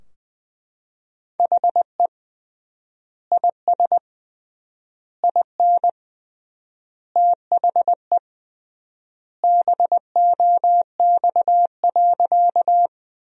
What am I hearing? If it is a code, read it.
Morse code, heard as a keyed tone, about 20 wpm, 700 Hz. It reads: HE IS IN THE BOX.
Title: 3x Word-Spacing - Sentences from Top 500 Words - 20wpm